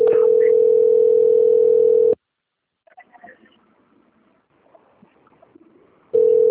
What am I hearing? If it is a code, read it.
Ringback tone (North America), 440 + 480 Hz.
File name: out- - - - - .28581